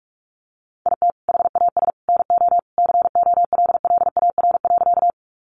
Morse code, heard as a keyed tone, about 44 wpm, 700 Hz. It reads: IT HAS NO COLLAR.
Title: Easy Phrases 45wpm